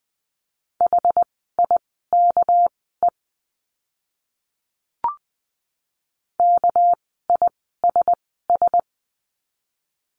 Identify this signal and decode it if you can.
Morse code, heard as a keyed tone, about 19 wpm, spaced as for 14 wpm, 700 Hz. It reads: HIKE KISS
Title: 199 Lesson 32f - Letter k - Words 20wpm